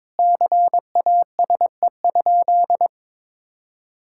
Morse code, heard as a keyed tone, about 22 wpm, 700 Hz. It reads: CASE?
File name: practice / 22wpm / Sentences with 4-Word Phrases - 22wpm